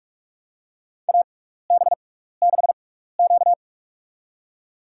Morse code, heard as a keyed tone, about 43 wpm, spaced as for 15 wpm, 700 Hz. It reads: AB6Q
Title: USA Call Signs - 4 Characters - 6x Character Spacing - 45wpm